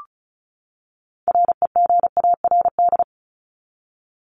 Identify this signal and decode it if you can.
Morse code, heard as a keyed tone, about 34 wpm, 700 Hz. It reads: REGARD
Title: ED Suffix - Top 100 Words with Base Words 35wpm